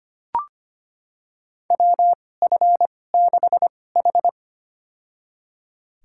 Morse code, heard as a keyed tone, about 25 wpm, spaced as for 18 wpm, 700 Hz. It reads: WF6H
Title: USA Call Signs - 4 Characters - 2x Character Spacing - 25wpm